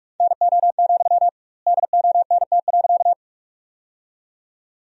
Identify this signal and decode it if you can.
Morse code, heard as a keyed tone, about 47 wpm, 700 Hz. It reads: NO, DONT.
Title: 2x Word-Spacing - Sentences from Top 300 Words - 45wpm